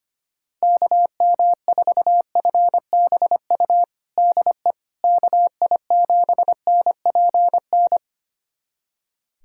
Morse code, heard as a keyed tone, about 25 wpm, 700 Hz. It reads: KM4FBU DE KI7NPN
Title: USA Callsign DE Callsign 25wpm